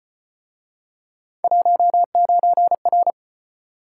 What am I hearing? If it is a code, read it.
Morse code, heard as a keyed tone, about 36 wpm, 700 Hz. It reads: 19R